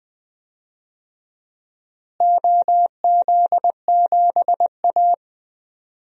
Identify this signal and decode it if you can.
Morse code, heard as a keyed tone, about 20 wpm, 700 Hz. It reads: OZ7A
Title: DXCC-100 Call Signs - 4 Characters - Repeated 3x - 20wpm